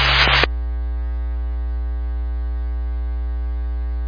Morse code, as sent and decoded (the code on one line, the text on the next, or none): none